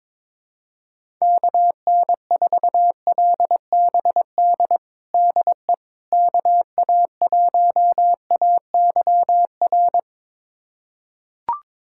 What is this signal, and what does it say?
Morse code, heard as a keyed tone, about 22 wpm, 700 Hz. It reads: KN4LBD DE KA1AYR
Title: USA Callsign DE Callsign - 6 Characters - 22wpm